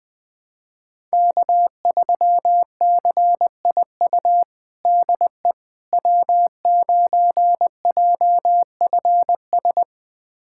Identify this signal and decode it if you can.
Morse code, heard as a keyed tone, about 20 wpm, 700 Hz. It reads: K3CIU DE W9JFS
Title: USA Callsign DE Callsign - 5 Characters - 20wpm